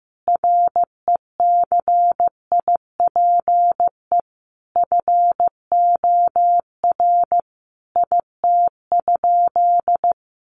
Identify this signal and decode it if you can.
Morse code, heard as a keyed tone, about 15 wpm, 700 Hz. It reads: RECIPE FOR IT?
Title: Sentences from Top 1200 Words 15wpm